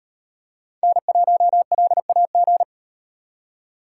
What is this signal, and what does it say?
Morse code, heard as a keyed tone, about 38 wpm, 700 Hz. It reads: N1LAG